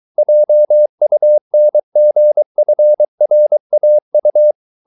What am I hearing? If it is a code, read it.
Morse code, heard as a keyed tone, about 23 wpm, 600 Hz. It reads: JUNGFRAU